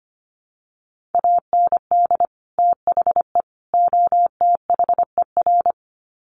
Morse code, heard as a keyed tone, about 25 wpm, 700 Hz. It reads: AND THE OTHER
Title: Sets of 3 Words 25wpm